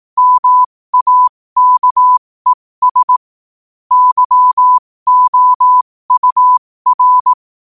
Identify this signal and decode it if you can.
Morse code, heard as a keyed tone, about 18 wpm, spaced as for 15 wpm, 1000 Hz. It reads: MAKES YOUR
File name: qotdfiles / 15wpm